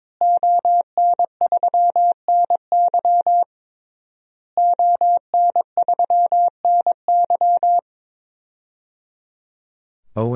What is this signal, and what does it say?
Morse code, heard as a keyed tone, about 22 wpm, 700 Hz. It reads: ON3NY ON3NY
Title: DXCC-100 Call Signs - 5 Characters - Repeated 3x - 22wpm